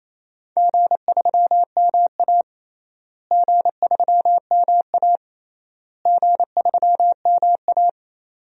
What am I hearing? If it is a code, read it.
Morse code, heard as a keyed tone, about 28 wpm, 700 Hz. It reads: G3MA G3MA G3MA
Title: DXCC-100 Call Signs - 4 Characters - Repeated 3x - 28wpm